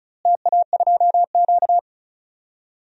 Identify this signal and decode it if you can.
Morse code, heard as a keyed tone, about 36 wpm, 700 Hz. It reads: TA2Q
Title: DXCC-100 Call Signs - 4 Characters - Repeated 2x - 35wpm